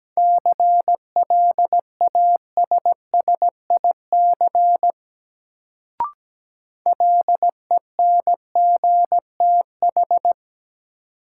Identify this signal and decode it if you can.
Morse code, heard as a keyed tone, about 17 wpm, 700 Hz. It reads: CLASSIC LENGTH
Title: Sentences from Top 1000 Words - New Word Drill - 17wpm